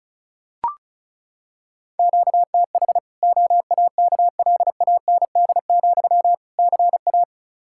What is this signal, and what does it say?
Morse code, heard as a keyed tone, about 36 wpm, 700 Hz. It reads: QTH OAKLAND, CA
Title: K1USN SST QSO Practice 35wpm